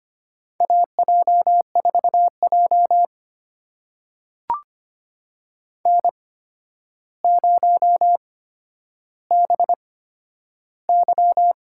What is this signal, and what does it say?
Morse code, heard as a keyed tone, about 25 wpm, spaced as for 11 wpm, 700 Hz. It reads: AJ4J N0BY
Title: USA Call Signs - 4 Characters - 8x Character Spacing - 25wpm